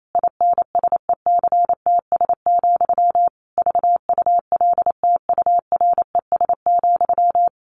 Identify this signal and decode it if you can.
Morse code, heard as a keyed tone, about 28 wpm, 700 Hz. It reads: INSECTS, VULTURES,
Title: A Sound Of Thunder 28wpm